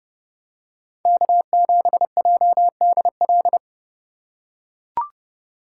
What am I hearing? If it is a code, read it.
Morse code, heard as a keyed tone, about 29 wpm, 700 Hz. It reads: K7JDL